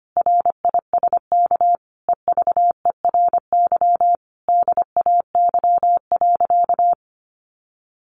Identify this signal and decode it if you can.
Morse code, heard as a keyed tone, about 25 wpm, 700 Hz. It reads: RISK EVERY DAY.
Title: Sentences from Top 1200 Words 25wpm